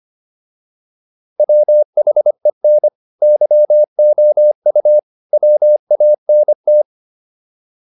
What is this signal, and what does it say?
Morse code, heard as a keyed tone, about 25 wpm, 600 Hz. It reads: WHEN YOU WANT